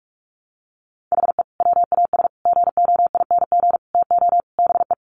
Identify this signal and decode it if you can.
Morse code, heard as a keyed tone, about 44 wpm, 700 Hz. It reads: HE WAS GOING TO BE
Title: Sets of 5 Words 45wpm